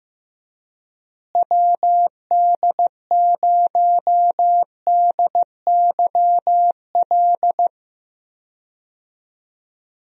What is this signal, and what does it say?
Morse code, heard as a keyed tone, about 15 wpm, 700 Hz. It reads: WD0DYL
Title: USA Call Signs - 6 Characters - Repeated 2x - 15wpm